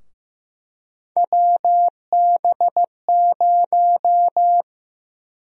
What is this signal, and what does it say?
Morse code, heard as a keyed tone, about 15 wpm, 700 Hz. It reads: WB0
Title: USA Call Sign Prefixes - 15wpm